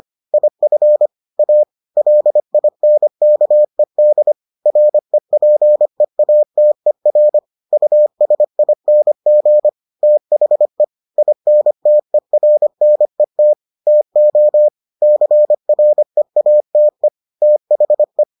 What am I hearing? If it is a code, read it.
Morse code, heard as a keyed tone, about 25 wpm, 600 Hz. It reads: IF A LINKED REPEATER USING THE INTERNET TO CREATE THE